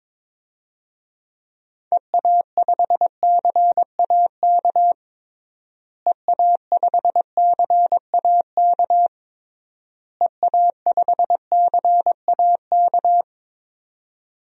Morse code, heard as a keyed tone, about 22 wpm, 700 Hz. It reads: EA5CAK EA5CAK EA5CAK
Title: DXCC-100 Call Signs - 6 Characters - Repeated 3x - 22wpm